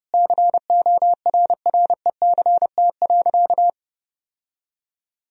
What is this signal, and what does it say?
Morse code, heard as a keyed tone, about 32 wpm, 700 Hz. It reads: CORRECT.
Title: Sentences from Top 1000 Words - Review Part 2 - 30wpm